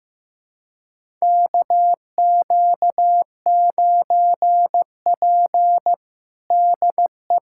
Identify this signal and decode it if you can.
Morse code, heard as a keyed tone, about 15 wpm, 700 Hz. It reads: KQ9P DE
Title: USA Callsign DE Callsign - 4 Characters - 15wpm